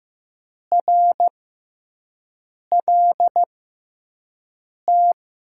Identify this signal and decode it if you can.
Morse code, heard as a keyed tone, about 15 wpm, spaced as for 5 wpm, 700 Hz. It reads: RLT
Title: USA Call Signs - 6 Characters - 6x Character Spacing - 15wpm